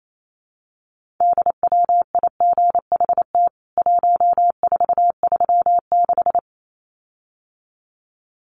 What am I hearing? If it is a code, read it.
Morse code, heard as a keyed tone, about 28 wpm, 700 Hz. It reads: DWIGHT 1436
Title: CWT Exchange 28wpm